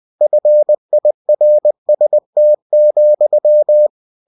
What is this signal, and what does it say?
Morse code, heard as a keyed tone, about 20 wpm, 600 Hz. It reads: FIRST,
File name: cw-books / winnie-the-pooh / winnie-20wpm-page-0079